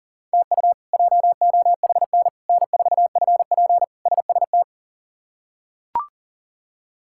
Morse code, heard as a keyed tone, about 40 wpm, 700 Hz. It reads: TU JOHN N4FP SST
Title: K1USN SST QSO Practice - 40wpm